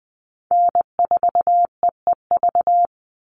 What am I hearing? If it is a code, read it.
Morse code, heard as a keyed tone, about 20 wpm, 700 Hz. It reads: N4EEV
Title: Call Signs - 20wpm